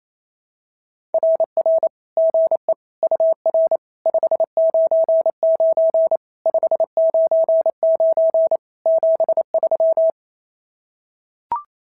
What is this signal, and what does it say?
Morse code, heard as a keyed tone, about 25 wpm, 650 Hz. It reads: RR GE UR 599 599 73
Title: SOTA QSO Practice 28wpm